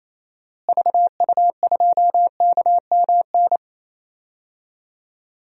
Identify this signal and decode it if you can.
Morse code, heard as a keyed tone, about 28 wpm, 700 Hz. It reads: VU2KMN
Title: DXCC-100 Call Signs - 6 Characters - Repeated 2x - 28wpm